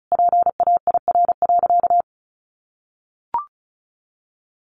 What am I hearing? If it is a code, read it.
Morse code, heard as a keyed tone, about 34 wpm, 700 Hz. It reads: PAIR.
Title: Sentences from Top 1200 Words 35wpm